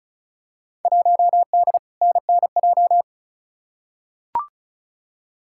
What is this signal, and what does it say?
Morse code, heard as a keyed tone, about 36 wpm, 700 Hz. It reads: 1D NNJ
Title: Field Day Exchanges - 35wpm